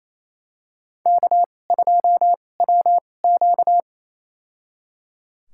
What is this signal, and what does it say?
Morse code, heard as a keyed tone, about 28 wpm, spaced as for 20 wpm, 700 Hz. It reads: K2WQ
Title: USA Call Signs - 2x Character Spacing - 28wpm